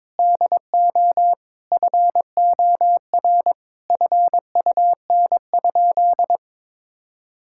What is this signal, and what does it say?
Morse code, heard as a keyed tone, about 22 wpm, 700 Hz. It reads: DO FOR FUN?
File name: practice / 22wpm / Sentences with 4-Word Phrases - 22wpm